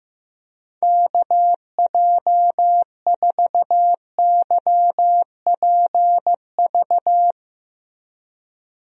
Morse code, heard as a keyed tone, about 15 wpm, 700 Hz. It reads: KJ4YPV
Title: USA Call Signs - 6 Characters - Repeated 3x - 15wpm